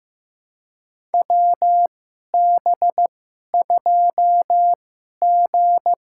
Morse code, heard as a keyed tone, about 15 wpm, spaced as for 11 wpm, 700 Hz. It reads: WB2G
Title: USA Call Signs - 2x Character Spacing - 15wpm